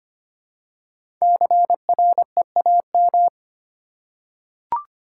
Morse code, heard as a keyed tone, about 25 wpm, 700 Hz. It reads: CREAM
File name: 5 Letter Words - First 3 Letters Given - 25wpm